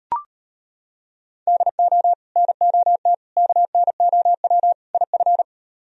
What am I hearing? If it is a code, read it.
Morse code, heard as a keyed tone, about 38 wpm, 700 Hz. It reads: DO NOT KNOW IF